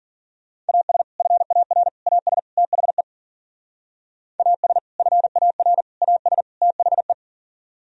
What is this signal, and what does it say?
Morse code, heard as a keyed tone, about 50 wpm, 700 Hz. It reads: AS FAR AS THE AS FAR AS THE